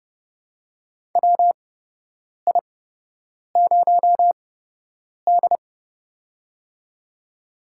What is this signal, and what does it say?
Morse code, heard as a keyed tone, about 30 wpm, spaced as for 8 wpm, 700 Hz. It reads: WI0D
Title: USA Call Signs - 4 Characters - 8x Character Spacing - 30wpm